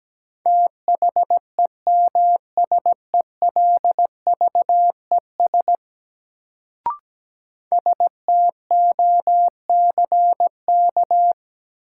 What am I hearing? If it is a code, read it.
Morse code, heard as a keyed tone, about 17 wpm, 700 Hz. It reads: THEMSELVES STOCK
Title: Sentences from Top 800 Words - New Word Drill - 17wpm